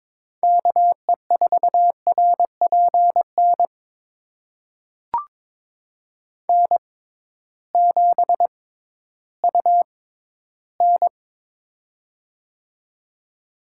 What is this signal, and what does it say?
Morse code, heard as a keyed tone, about 22 wpm, spaced as for 13 wpm, 700 Hz. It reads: KE4RPN N7UN
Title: USA Call Signs - 6x Character Spacing - 22wpm